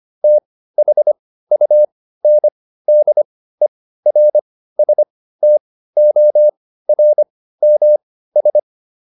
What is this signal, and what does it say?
Morse code, heard as a keyed tone, about 25 wpm, spaced as for 15 wpm, 600 Hz. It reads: THUNDERSTORMS